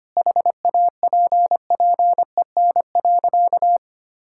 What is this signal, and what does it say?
Morse code, heard as a keyed tone, about 25 wpm, 700 Hz. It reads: HAPPEN.
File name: practice / 25wpm / Sentences with 9-Word Phrases - 25wpm